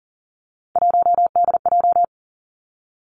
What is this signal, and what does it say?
Morse code, heard as a keyed tone, about 39 wpm, 700 Hz. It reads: 1DJ
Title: Three Challenging Letter-Number Combinations 40wpm